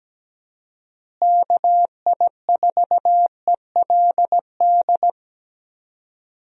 Morse code, heard as a keyed tone, about 17 wpm, 700 Hz. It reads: KI4ELD
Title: USA Call Signs - 6 Characters - Repeated 3x - 17wpm